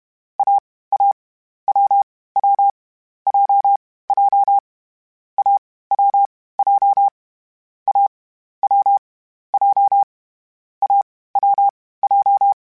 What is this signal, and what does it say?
Morse code, heard as a keyed tone, about 31 wpm, 800 Hz. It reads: A A W W J J A W J A W J A W J
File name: Difficult Combos / Difficult Combos-mp3s / AWJ1 @30